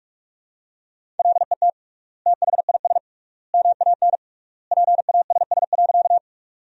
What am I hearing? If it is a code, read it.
Morse code, heard as a keyed tone, about 44 wpm, 700 Hz. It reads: LET THIS MAN PASS.
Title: 3x Word-Spacing - Sentences from Top 400 Words - 45wpm